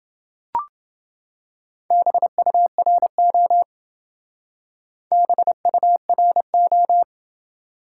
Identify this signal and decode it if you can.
Morse code, heard as a keyed tone, about 28 wpm, 700 Hz. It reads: BURO BURO